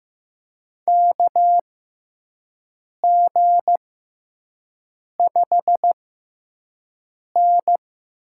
Morse code, heard as a keyed tone, about 15 wpm, spaced as for 5 wpm, 700 Hz. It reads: KG5N
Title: USA Call Signs - 6 Characters - 6x Character Spacing - 15wpm